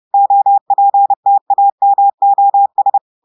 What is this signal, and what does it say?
Morse code, heard as a keyed tone, about 30 wpm, 800 Hz. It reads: OPTAMOS
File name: audios / portugais / portugais texte en clair 1 vitesse 30 mots minutes